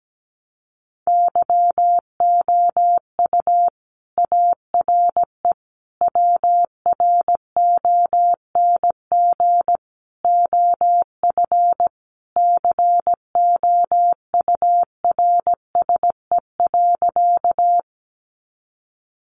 Morse code, heard as a keyed tone, about 17 wpm, 700 Hz. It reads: YOU ARE WRONG OF COURSE.
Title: Sentences from Top 900 Words 17wpm